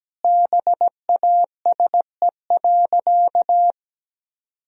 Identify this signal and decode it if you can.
Morse code, heard as a keyed tone, about 17 wpm, 700 Hz. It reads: BASE.
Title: Sentences from Top 1000 Words - Review Part 1 - 17wpm